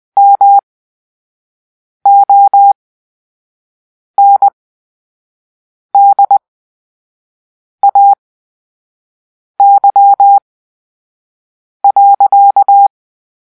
Morse code, heard as a keyed tone, about 20 wpm, spaced as for 5 wpm, 800 Hz. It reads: MONDAY.
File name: Grp-3 Drill-4